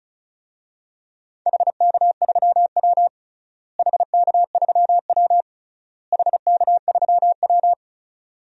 Morse code, heard as a keyed tone, about 36 wpm, 700 Hz. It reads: HK3W HK3W HK3W
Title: DXCC-100 Call Signs - 4 Characters - Repeated 3x - 35wpm